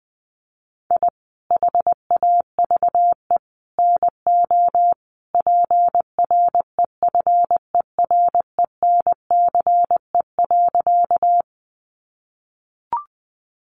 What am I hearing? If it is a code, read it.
Morse code, heard as a keyed tone, about 20 wpm, 700 Hz. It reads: I HAVE NO PREFERENCE.
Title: Sentences from Top 1500 Words 20wpm